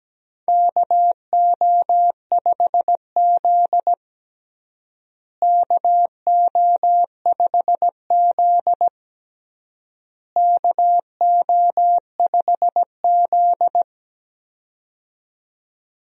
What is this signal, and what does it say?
Morse code, heard as a keyed tone, about 17 wpm, 700 Hz. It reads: KO5Z KO5Z KO5Z
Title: USA Call Signs - 4 Characters - Repeated 3x - 17wpm